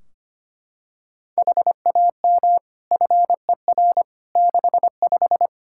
Morse code, heard as a keyed tone, about 25 wpm, 700 Hz. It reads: HAM FER 65
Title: Practice QSO - Exchange 3 - More Info - 25wpm